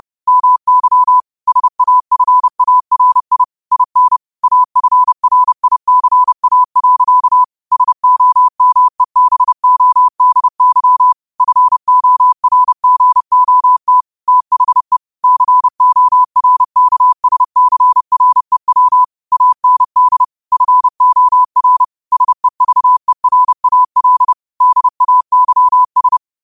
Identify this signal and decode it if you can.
Morse code, heard as a keyed tone, about 30 wpm, 1000 Hz. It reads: MY SAFARI IN AFRICA. SOMEBODY FORGOT THE CORKSCREW AND FOR SEVERAL DAYS